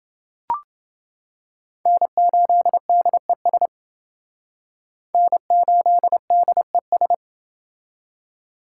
Morse code, heard as a keyed tone, about 28 wpm, 700 Hz. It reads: N8DES N8DES